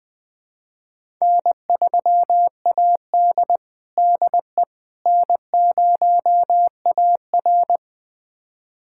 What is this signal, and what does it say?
Morse code, heard as a keyed tone, about 20 wpm, 700 Hz. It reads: N3AD DE N0AR
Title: USA Callsign DE Callsign - 4 Characters - 20wpm